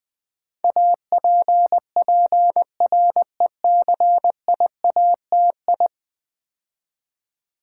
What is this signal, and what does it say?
Morse code, heard as a keyed tone, about 21 wpm, 700 Hz. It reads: APPRECIATI
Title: Build A Word - 5 Letters At A Time - 12-Letter Words - 20wpm